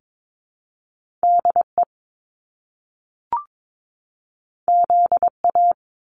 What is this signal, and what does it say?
Morse code, heard as a keyed tone, about 22 wpm, 700 Hz. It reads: DE ZA
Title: Top Two Letter Combinations in Words 22wpm